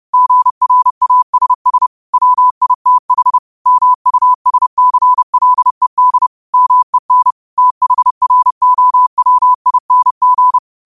Morse code, heard as a keyed tone, about 30 wpm, 1000 Hz. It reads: GRASS WITH MUSCLED MEN THROWING
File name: qotdfiles / 30wpm